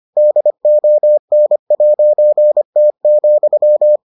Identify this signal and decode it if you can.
Morse code, heard as a keyed tone, about 25 wpm, 600 Hz. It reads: DON'T,